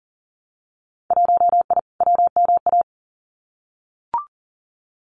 Morse code, heard as a keyed tone, about 40 wpm, 700 Hz. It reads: 1I WMA
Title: Winter Field Day Exchanges Encore 40wpm